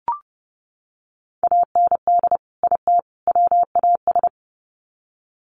Morse code, heard as a keyed tone, about 30 wpm, 700 Hz. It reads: AND IT WAS
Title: Sets of 3 Words 30wpm